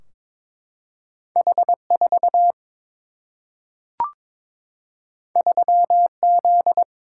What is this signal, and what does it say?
Morse code, heard as a keyed tone, about 22 wpm, 700 Hz. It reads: H 4 3 Z
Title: Single-Letter Single-Number 22wpm